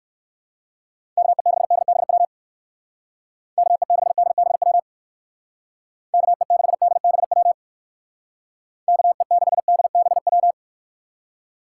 Morse code, heard as a keyed tone, about 56 wpm, 700 Hz. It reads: KE6DBW KE6DBW KE6DBW KE6DBW
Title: Speed Racing - USA 2x3 Call Signs 45wpm